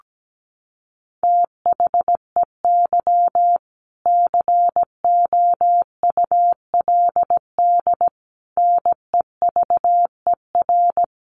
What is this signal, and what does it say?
Morse code, heard as a keyed tone, about 17 wpm, 700 Hz. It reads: THEY COULD NEVER
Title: Sentences from Top 1200 Words 17wpm